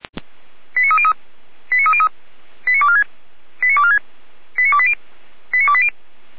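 Five-tone selective call (ZVEI), heard as quick runs of five tones, tones of about 70 ms. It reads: 89303 89303 89246 89246 89280 89280
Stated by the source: ZVEI five-tone selective call 89229 received by SDR.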